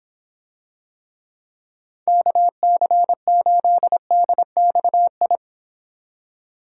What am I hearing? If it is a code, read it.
Morse code, heard as a keyed tone, about 26 wpm, 700 Hz. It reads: KC8DXI